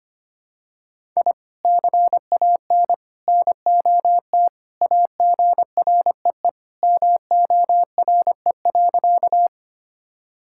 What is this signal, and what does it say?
Morse code, heard as a keyed tone, about 25 wpm, 700 Hz. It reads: I CAN NOT AGREE MORE.
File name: practice / 25wpm / Sentences with 3-Word Phrases - 25wpm